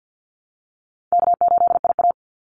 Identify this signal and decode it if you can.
Morse code, heard as a keyed tone, about 50 wpm, 700 Hz. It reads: K8IA
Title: Easy Call Signs - 50wpm